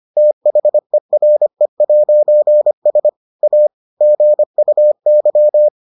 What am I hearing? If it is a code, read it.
Morse code, heard as a keyed tone, about 25 wpm, 600 Hz. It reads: THERE'S A GUY